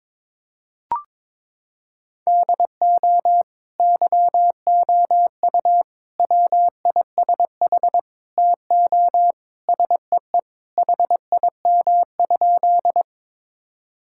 Morse code, heard as a keyed tone, about 22 wpm, 700 Hz. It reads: DO YOU WISH TO SEE HIM?
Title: Sentences from Top 1600 Words 22wpm